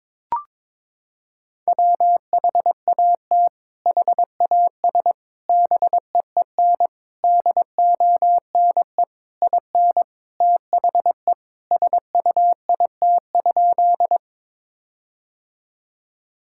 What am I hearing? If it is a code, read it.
Morse code, heard as a keyed tone, about 22 wpm, 700 Hz. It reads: WHAT HAS BEEN DONE IN THE SUIT?
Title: Sentences from Top 1600 Words 22wpm